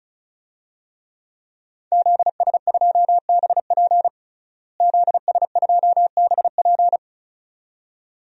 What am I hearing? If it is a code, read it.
Morse code, heard as a keyed tone, about 36 wpm, 700 Hz. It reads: ZS2BP ZS2BP
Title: DXCC-100 Call Signs - 5 Characters - Repeated 2x - 35wpm